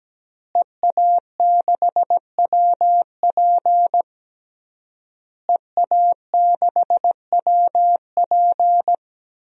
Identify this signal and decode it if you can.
Morse code, heard as a keyed tone, about 17 wpm, 700 Hz. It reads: EA6WP EA6WP
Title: DXCC-100 Call Signs - 5 Characters - Repeated 3x - 17wpm